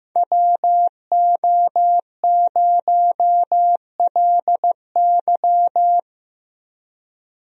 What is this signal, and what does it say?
Morse code, heard as a keyed tone, about 16 wpm, 700 Hz. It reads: WO0LY
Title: USA Call Signs - 5 Characters - Repeated 3x - 15wpm